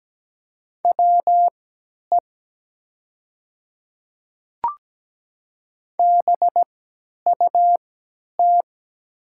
Morse code, heard as a keyed tone, about 17 wpm, spaced as for 10 wpm, 690 Hz. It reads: WE BUT
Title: 3x Character Spacing - Top 100 Words - 17wpm